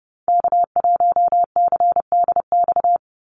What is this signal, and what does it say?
Morse code, heard as a keyed tone, about 30 wpm, 700 Hz. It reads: K1CDX
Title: Call Signs - 30wpm